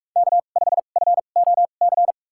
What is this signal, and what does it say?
Morse code, heard as a keyed tone, about 47 wpm, spaced as for 33 wpm, 700 Hz. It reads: K5FOC
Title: USA Call Signs - 5 Characters - 2x Character Spacing - 45wpm